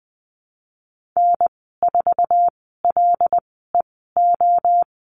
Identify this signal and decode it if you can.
Morse code, heard as a keyed tone, about 20 wpm, spaced as for 15 wpm, 700 Hz. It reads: N4LEO
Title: 088 Lesson 15g - Number 5 - Call Signs 20wpm